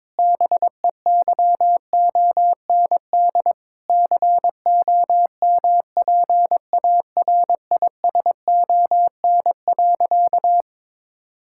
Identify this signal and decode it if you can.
Morse code, heard as a keyed tone, about 22 wpm, 700 Hz. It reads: BEYOND COMPARISON.
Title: Sentences from Top 1600 Words 22wpm